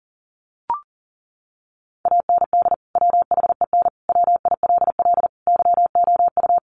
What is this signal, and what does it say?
Morse code, heard as a keyed tone, about 40 wpm, 700 Hz. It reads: AND WHEN WILL YOU
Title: Sentences from Top 100 Words 40wpm